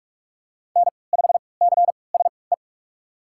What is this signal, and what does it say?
Morse code, heard as a keyed tone, about 47 wpm, spaced as for 26 wpm, 700 Hz. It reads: N5CSE
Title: USA Call Signs - 5 Characters - 3x Character Spacing - 45wpm